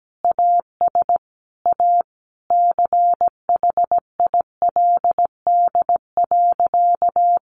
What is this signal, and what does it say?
Morse code, heard as a keyed tone, about 17 wpm, 700 Hz. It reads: AS A CHILD.
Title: Sentences from Top 1500 Words 17wpm